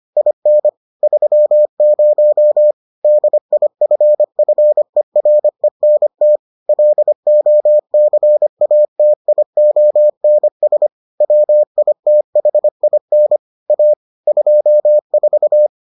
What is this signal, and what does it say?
Morse code, heard as a keyed tone, about 25 wpm, 600 Hz. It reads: IN 30 DIFFERENT LOCATIONS WITHIN A 24